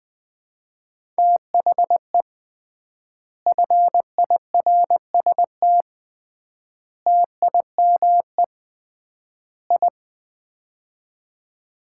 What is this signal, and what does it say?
Morse code, heard as a keyed tone, about 20 wpm, 700 Hz. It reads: THE FIRST TIME I
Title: 3x Word-Spacing - Sets of 4 Words - 20wpm